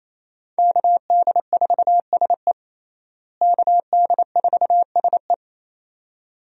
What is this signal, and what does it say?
Morse code, heard as a keyed tone, about 28 wpm, 700 Hz. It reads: KD4SE KD4SE
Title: USA Call Signs - 5 Characters - Repeated 2x - 28wpm